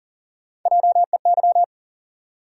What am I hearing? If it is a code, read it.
Morse code, heard as a keyed tone, about 40 wpm, 700 Hz. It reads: JEY
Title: USA Call Sign Postfixes - 3 Letters 40wpm